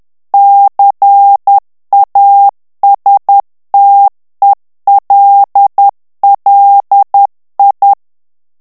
Morse code, heard as a keyed tone, about 10 wpm, 800 Hz. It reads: CASTELLI
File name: morse mp3 / ITALIEN texte clair 1 vitesse 10 mots minute